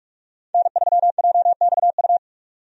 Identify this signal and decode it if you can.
Morse code, heard as a keyed tone, about 46 wpm, 700 Hz. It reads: N3JXU
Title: USA Call Signs - 5 Characters - 45wpm